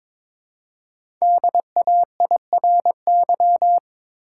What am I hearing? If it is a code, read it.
Morse code, heard as a keyed tone, about 22 wpm, 700 Hz. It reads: DAIRY
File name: 5 Letter Words - First 3 Letters Given - 22wpm